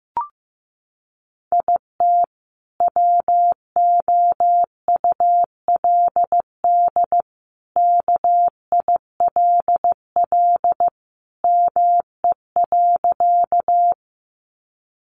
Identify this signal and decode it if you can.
Morse code, heard as a keyed tone, about 15 wpm, 700 Hz. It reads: IT WOULD KILL ME.
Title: Sentences from Top 1400 Words 15wpm